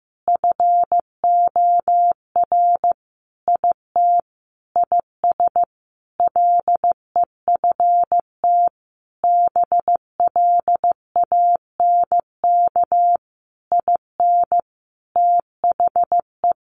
Morse code, heard as a keyed tone, about 15 wpm, 700 Hz. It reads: FOR IT IS LEFT BLANK IN THE
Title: Sentences from Top 1500 Words 15wpm